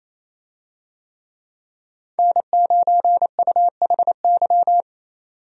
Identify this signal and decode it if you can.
Morse code, heard as a keyed tone, about 27 wpm, 700 Hz. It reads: N9UHY
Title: USA Call Signs - 5 Characters - 28wpm